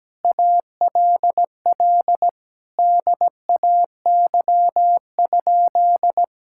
Morse code, heard as a keyed tone, about 17 wpm, 700 Hz. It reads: ALL DAY?
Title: Sentences from Top 1000 Words - Review Part 5 - 17wpm